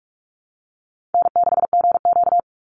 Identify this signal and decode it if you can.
Morse code, heard as a keyed tone, about 44 wpm, 700 Hz. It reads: N6GQ
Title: Easy Call Signs Repeated 3x - 45wpm